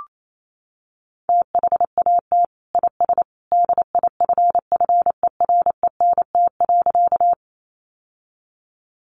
Morse code, heard as a keyed tone, about 28 wpm, 700 Hz. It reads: THAT IS DIFFERENT.
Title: Sentences from Top 600 Words 28wpm